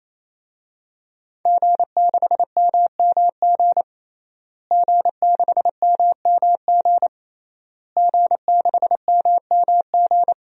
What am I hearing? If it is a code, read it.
Morse code, heard as a keyed tone, about 28 wpm, 700 Hz. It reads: G6MMG G6MMG G6MMG
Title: DXCC-100 Call Signs - 5 Characters - Repeated 3x - 28wpm